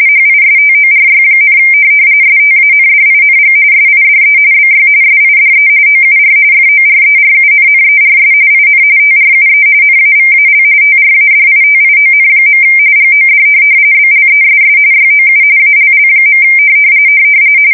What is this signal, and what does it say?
RTTY, about 45.45 baud. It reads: HISTORY, STANDING SIDE BY SIDE OUR FORERUNNERS IN TRUE NAVAL TRADITION. GOD BLESS YOU ALL.
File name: fsk170 / USS-Missouri